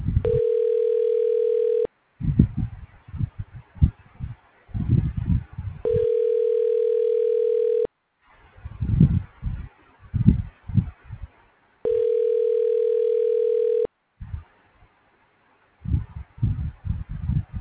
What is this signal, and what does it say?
Ringback tone (North America), 440 + 480 Hz.
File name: recordings / RE0316a857a17943bb92ba3929febbefc8